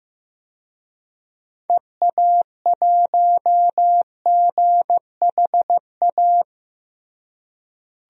Morse code, heard as a keyed tone, about 15 wpm, 700 Hz. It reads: EA1GHA
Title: DXCC-100 Call Signs - 6 Characters - Repeated 3x - 15wpm